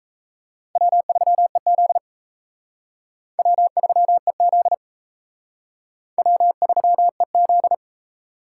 Morse code, heard as a keyed tone, about 35 wpm, 700 Hz. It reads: W3EZ W3EZ W3EZ
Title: Speed Racing - USA Call Signs 28wpm